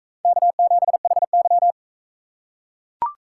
Morse code, heard as a keyed tone, about 38 wpm, 700 Hz. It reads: K7HY